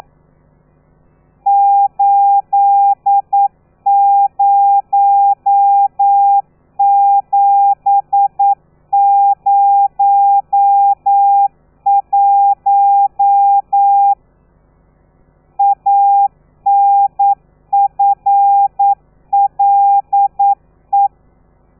Morse code, heard as a keyed tone, about 9 wpm, 800 Hz. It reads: 80701 ANFLE
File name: telegrafia / i2tfj / Lez13 40